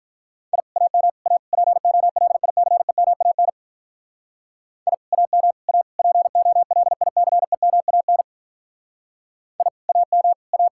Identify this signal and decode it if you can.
Morse code, heard as a keyed tone, about 42 wpm, 700 Hz. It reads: I AM A POLICEMAN I AM A POLICEMAN I AM A
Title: Speed-Racing Common QSO Phrases Encore - 35wpm